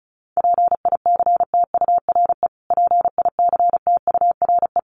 Morse code, heard as a keyed tone, about 34 wpm, 700 Hz. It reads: PICTURE PICTURE
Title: Top 200 Words Repeated 35wpm